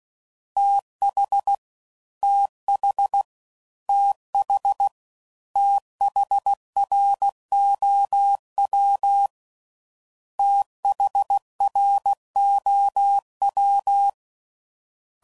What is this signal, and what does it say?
Morse code, heard as a keyed tone, about 16 wpm, 800 Hz. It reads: TH TH TH THROW THROW
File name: Prefixes / FLs & THs / TH Prefixes 3